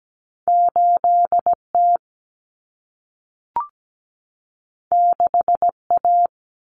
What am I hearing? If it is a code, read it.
Morse code, heard as a keyed tone, about 17 wpm, 700 Hz. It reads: 8T 6A
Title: Common Cut Numbers - 2-Digit 17wpm